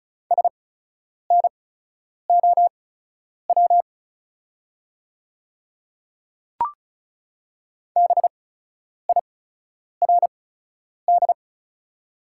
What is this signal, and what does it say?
Morse code, heard as a keyed tone, about 36 wpm, spaced as for 10 wpm, 700 Hz. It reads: SNOW BIRD
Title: 8x Character Spacing - Top 500 Words - 35wpm